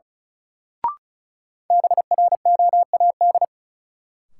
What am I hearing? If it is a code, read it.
Morse code, heard as a keyed tone, about 36 wpm, 700 Hz. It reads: BROAD
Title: Sentences from Top 1000 Words - New Word Drill - 35wpm